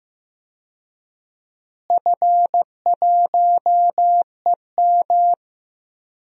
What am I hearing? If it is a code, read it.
Morse code, heard as a keyed tone, about 15 wpm, 700 Hz. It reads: F1EM
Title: DXCC-100 Call Signs - 4 Characters - Repeated 3x - 15wpm